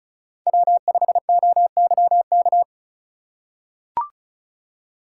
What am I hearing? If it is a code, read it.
Morse code, heard as a keyed tone, about 34 wpm, 700 Hz. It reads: W5OYK
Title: USA Call Signs - 5 Characters - 3x Character Spacing - 35wpm